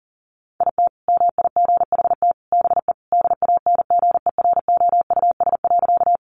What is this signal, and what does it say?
Morse code, heard as a keyed tone, about 40 wpm, 700 Hz. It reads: IT MIGHT BE DANGEROUS.
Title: Sentences from Top 1200 Words 40wpm